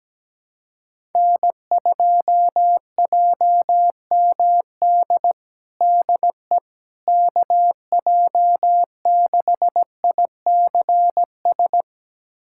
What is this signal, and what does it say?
Morse code, heard as a keyed tone, about 17 wpm, 700 Hz. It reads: N2JMD DE KJ6ICS
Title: USA Callsign DE Callsign 17wpm